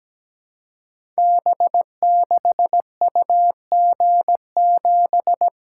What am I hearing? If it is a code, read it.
Morse code, heard as a keyed tone, about 17 wpm, 700 Hz. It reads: B6UG7
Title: Code Groups - 5 Letter:Numbers - 17wpm